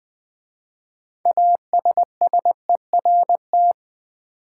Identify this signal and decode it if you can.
Morse code, heard as a keyed tone, about 20 wpm, 700 Hz. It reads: ASSERT
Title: 6 Letter Words - First 3 Letters Given - 20wpm